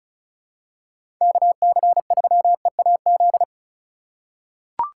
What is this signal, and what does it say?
Morse code, heard as a keyed tone, about 36 wpm, 700 Hz. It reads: KC3EAZ
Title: USA Call Signs - 6 Characters - 4x Character Spacing - 35wpm